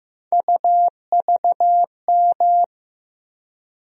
Morse code, heard as a keyed tone, about 15 wpm, 700 Hz. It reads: UVM
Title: USA Call Sign Postfixes - 3 Letters 15wpm